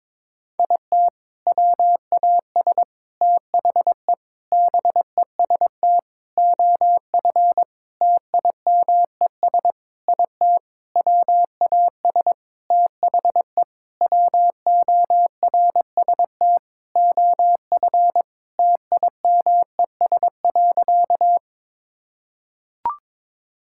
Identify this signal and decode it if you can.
Morse code, heard as a keyed tone, about 22 wpm, 700 Hz. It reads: IT WAS THE BEST OF TIMES IT WAS THE WORST OF TIMES.
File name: practice / 22wpm / Sentences with 8-Word Phrases - 22wpm